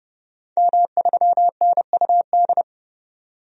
Morse code, heard as a keyed tone, about 31 wpm, 690 Hz. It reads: M3NUD
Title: DXCC-100 Call Signs - 5 Characters - 30wpm